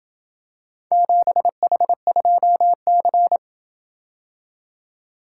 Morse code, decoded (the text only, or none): 7H2C